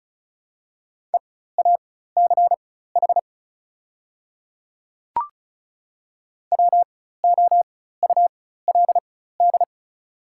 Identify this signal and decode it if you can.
Morse code, heard as a keyed tone, about 34 wpm, spaced as for 16 wpm, 700 Hz. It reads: EACH WOULD
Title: 4x Character Spacing - Top 100 Words - 35wpm